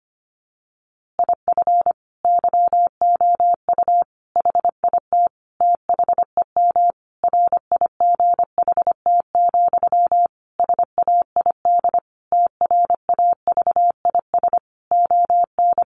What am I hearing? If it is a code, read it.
Morse code, heard as a keyed tone, about 25 wpm, 700 Hz. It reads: IF YOU HIT THEM RIGHT, SAID TRAVIS ON
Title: A Sound Of Thunder 25wpm